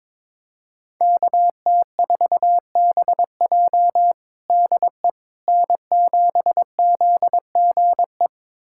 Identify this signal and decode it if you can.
Morse code, heard as a keyed tone, about 22 wpm, 700 Hz. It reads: KT4BJ DE N7ZGE
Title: USA Callsign DE Callsign - 5 Characters - 22wpm